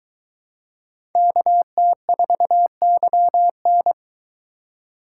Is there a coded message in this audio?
Yes: Morse code from 1.2 to 3.9 s, heard as a keyed tone, about 23 wpm, 700 Hz. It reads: KT4YN